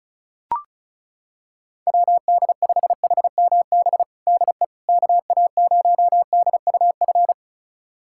Morse code, heard as a keyed tone, about 36 wpm, 700 Hz. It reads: WD5HMB DE KA0DUF
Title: USA Callsign DE Callsign 35wpm